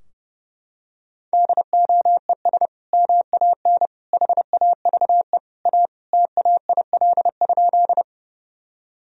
Morse code, heard as a keyed tone, about 32 wpm, 700 Hz. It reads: DOES MAN HAVE A TAIL?
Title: Sentences from Top 1000 Words - Review Part 3 - 30wpm